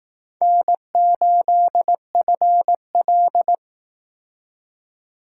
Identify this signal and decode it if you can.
Morse code, heard as a keyed tone, about 18 wpm, 700 Hz. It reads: N8FL
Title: Speed Racing - USA Call Signs 15wpm